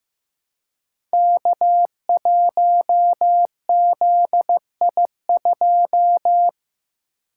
Morse code, heard as a keyed tone, about 15 wpm, 700 Hz. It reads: K1ZI2
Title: Code Groups - 5 Letter:Numbers - 15wpm